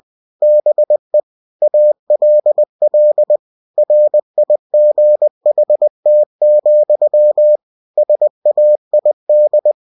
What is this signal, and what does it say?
Morse code, heard as a keyed tone, about 20 wpm, 600 Hz. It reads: BE ALL RIGHT, SAID
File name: cw-books / winnie-the-pooh / winnie-20wpm-page-0024